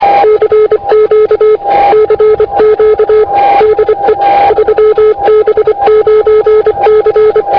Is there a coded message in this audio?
Yes: Morse code from 0.2 to 7.4 s, heard as a keyed tone, about 24 wpm, 430 Hz, with radio noise. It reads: CQ CQ DE 3B9C